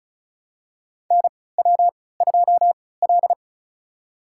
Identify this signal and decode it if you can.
Morse code, heard as a keyed tone, about 36 wpm, spaced as for 20 wpm, 700 Hz. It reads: NW2L
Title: USA Call Signs - 3x Character Spacing - 35wpm